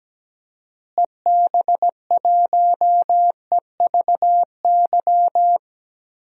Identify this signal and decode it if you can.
Morse code, heard as a keyed tone, about 17 wpm, 700 Hz. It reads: EB1EVY
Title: DXCC-100 Call Signs - 6 Characters - Repeated 3x - 17wpm